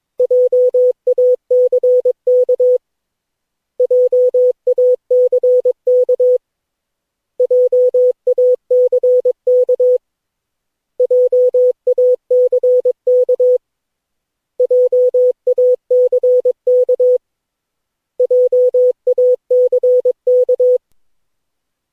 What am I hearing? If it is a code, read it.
Morse code, heard as a keyed tone, about 22 wpm, 500 Hz. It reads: JACK JACK JACK JACK JACK JACK
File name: cw / Session9 cw training